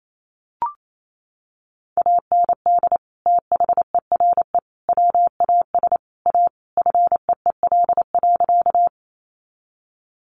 Morse code, heard as a keyed tone, about 28 wpm, 700 Hz. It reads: AND THERE WAS A FEEL.
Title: A Sound Of Thunder 28wpm